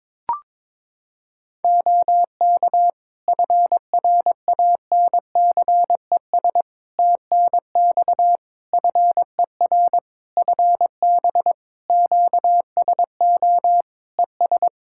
Morse code, heard as a keyed tone, about 22 wpm, 700 Hz. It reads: OK FRANCES TNX FER FB QSO ES
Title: Practice QSO - Exchange 4 - Ending - 22wpm